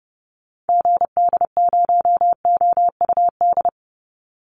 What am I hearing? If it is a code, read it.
Morse code, heard as a keyed tone, about 30 wpm, 700 Hz. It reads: GD0OUD
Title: Call Signs Repeated 2x - 30wpm